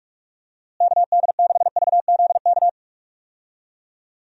Morse code, heard as a keyed tone, about 46 wpm, 700 Hz. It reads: KD6VZK
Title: USA Call Signs - 6 Characters - Repeated 3x - 45wpm